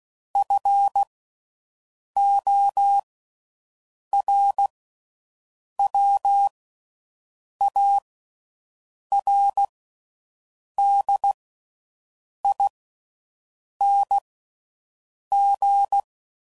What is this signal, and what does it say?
Morse code, heard as a keyed tone, about 16 wpm, spaced as for 6 wpm, 800 Hz. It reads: FORWARDING